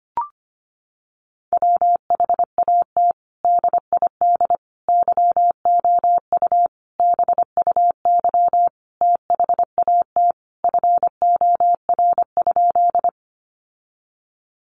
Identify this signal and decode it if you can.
Morse code, heard as a keyed tone, about 25 wpm, 700 Hz. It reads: WHAT DID YOU BUY THAT FOR?
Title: Sentences from Top 700 Words 25wpm